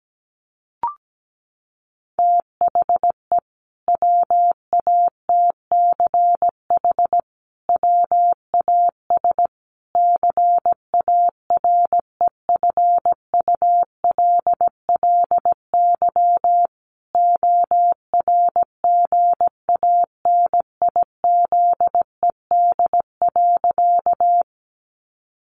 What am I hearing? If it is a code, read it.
Morse code, heard as a keyed tone, about 17 wpm, 700 Hz. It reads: THE WATCH WAS CAREFULLY ORGANIZED.
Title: Sentences from Top 1300 Words 17wpm